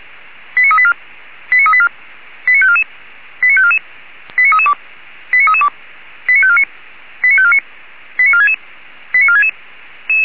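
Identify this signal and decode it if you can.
Five-tone selective call (ZVEI), heard as quick runs of five tones, tones of about 70 ms. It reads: 89384 89384 89544 89544 89332 89332 89549 89549 89466 89466
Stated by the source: Five-tone ZVEI selective call received on SDR.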